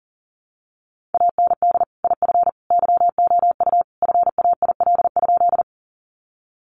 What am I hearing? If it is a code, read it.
Morse code, heard as a keyed tone, about 40 wpm, 700 Hz. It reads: AND IF YOU FAIL?
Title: Sentences from Top 1100 Words 40wpm